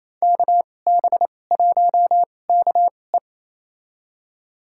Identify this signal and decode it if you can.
Morse code, heard as a keyed tone, about 28 wpm, spaced as for 20 wpm, 700 Hz. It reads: KB1KE
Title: USA Call Signs - 5 Characters - 2x Character Spacing - 28wpm